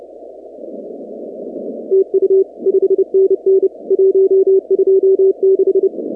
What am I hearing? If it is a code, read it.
Morse code, heard as a keyed tone, about 29 wpm, 380 Hz. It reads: TU 5NN 126
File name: contests / 2019-cw-wpx-kt5j-k5pi- -2330 0-46